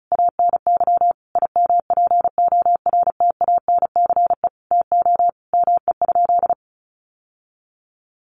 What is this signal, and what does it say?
Morse code, heard as a keyed tone, about 34 wpm, 700 Hz. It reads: ANY IMPORTANCE TO ME?
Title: Sentences from Top 1200 Words 35wpm